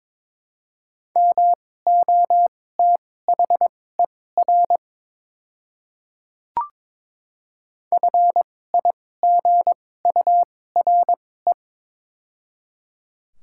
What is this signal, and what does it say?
Morse code, heard as a keyed tone, about 22 wpm, spaced as for 16 wpm, 700 Hz. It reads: MOTHER FIGURE
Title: 2x Character Spacing - Top 500 Words - 22wpm